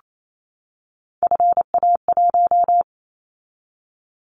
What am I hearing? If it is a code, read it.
Morse code, heard as a keyed tone, about 28 wpm, 700 Hz. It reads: FA1
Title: Callsign Prefixes for Top 50 DXCC - 28wpm